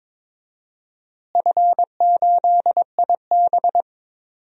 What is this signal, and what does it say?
Morse code, heard as a keyed tone, about 22 wpm, 700 Hz. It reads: F8IB
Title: DXCC-100 Call Signs - 4 Characters - Repeated 3x - 22wpm